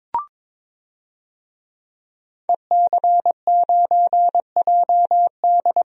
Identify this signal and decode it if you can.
Morse code, heard as a keyed tone, about 21 wpm, 700 Hz. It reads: EC9JD
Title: DXCC-100 Call Signs - 5 Characters - 22wpm